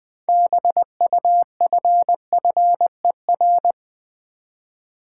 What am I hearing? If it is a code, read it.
Morse code, heard as a keyed tone, about 20 wpm, 700 Hz. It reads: BUFFER
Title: 6 Letter Words - First 3 Letters Given - 20wpm